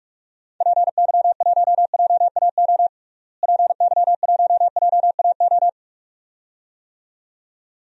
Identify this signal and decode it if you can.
Morse code, heard as a keyed tone, about 46 wpm, 700 Hz. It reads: PY1JAO PY1JAO
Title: DXCC-100 Call Signs - 6 Characters - Repeated 2x - 45wpm